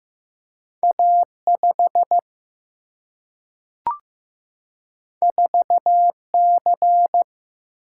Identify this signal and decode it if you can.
Morse code, heard as a keyed tone, about 15 wpm, 700 Hz. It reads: A 5 4 C
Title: Single-Letter Single-Number 15wpm